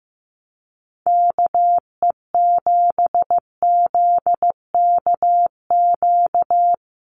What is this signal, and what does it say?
Morse code, heard as a keyed tone, about 15 wpm, 700 Hz. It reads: KE7ZKQ
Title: USA Call Signs Repeated 2x 15wpm